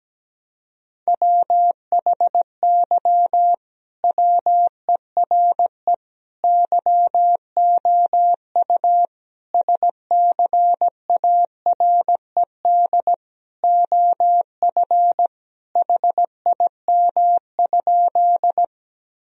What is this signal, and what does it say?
Morse code, heard as a keyed tone, about 17 wpm, 700 Hz. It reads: WHY WERE YOU SCARED OF HIM?
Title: Sentences from Top 1000 Words - Review Part 4 - 17wpm